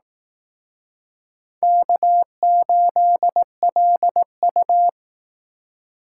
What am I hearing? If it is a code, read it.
Morse code, heard as a keyed tone, about 18 wpm, 700 Hz. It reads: K8LU
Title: Speed Racing - USA 1x2 Call Signs 15wpm